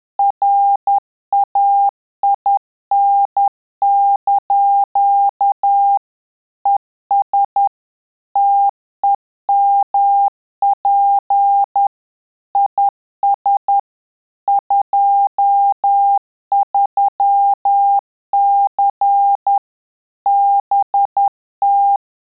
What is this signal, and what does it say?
Morse code, heard as a keyed tone, about 10 wpm, 800 Hz. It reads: RAIN) ES TEMP IS 23C BT
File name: Telegraphy-group / QSO CW 50 CPM